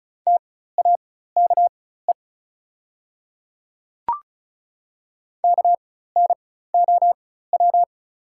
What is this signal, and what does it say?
Morse code, heard as a keyed tone, about 34 wpm, spaced as for 16 wpm, 700 Hz. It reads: TAKE KNOW
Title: 4x Character Spacing - Top 300 Words - 35wpm